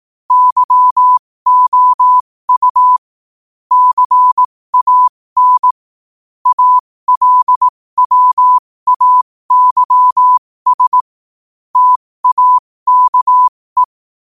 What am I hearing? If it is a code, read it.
Morse code, heard as a keyed tone, about 18 wpm, spaced as for 15 wpm, 1000 Hz. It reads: YOU CAN ALWAYS TAKE